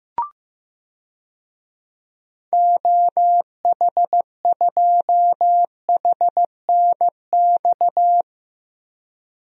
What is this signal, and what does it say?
Morse code, heard as a keyed tone, about 16 wpm, 700 Hz. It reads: OH2HNX
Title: DXCC-100 Call Signs - 6 Characters - 15wpm